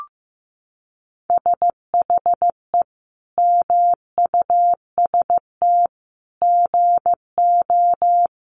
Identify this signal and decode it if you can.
Morse code, heard as a keyed tone, about 15 wpm, 700 Hz. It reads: SHE MUST GO
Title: Sentences from Top 1200 Words 15wpm